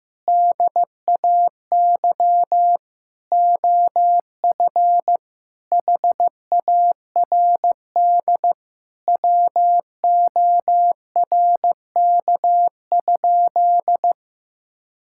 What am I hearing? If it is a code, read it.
Morse code, heard as a keyed tone, about 16 wpm, 700 Hz. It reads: DAY OF HARD WORK?
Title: Sentences from Top 1000 Words - Review Part 6 - 15wpm